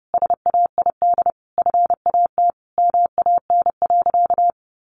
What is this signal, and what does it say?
Morse code, heard as a keyed tone, about 30 wpm, 700 Hz. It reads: SAID FAT MAN.
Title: Sentences from Top 1100 Words 30wpm